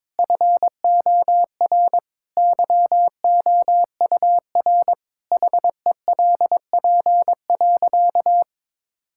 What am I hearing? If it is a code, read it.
Morse code, heard as a keyed tone, about 22 wpm, 700 Hz. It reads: FOR YOUR HELP.
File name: practice / 22wpm / Sentences with 4-Word Phrases - 22wpm